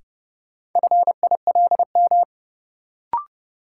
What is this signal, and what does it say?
Morse code, heard as a keyed tone, about 30 wpm, 700 Hz. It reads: FILM
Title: Sentences from Top 1000 Words - Review Part 3 - New Word Drill - 30wpm